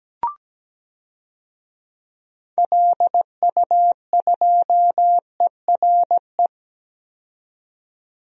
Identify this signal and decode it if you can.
Morse code, heard as a keyed tone, about 17 wpm, 700 Hz. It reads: LU2ERE
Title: DXCC-100 Call Signs - 6 Characters - 17wpm